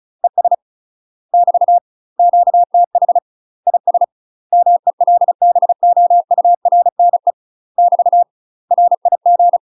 Morse code, heard as a keyed tone, about 37 wpm, 700 Hz. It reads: ES <BT> QTH IS MELBOURNE <BT> RIG
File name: UK MASTER QSOs TRAINING / 35WPM / UK MASTER QSOs 0003